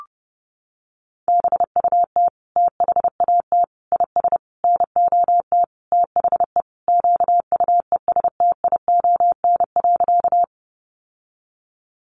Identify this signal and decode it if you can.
Morse code, heard as a keyed tone, about 30 wpm, 700 Hz. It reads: BUT THAT IS NOT THE QUESTION.
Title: Sentences from Top 600 Words 30wpm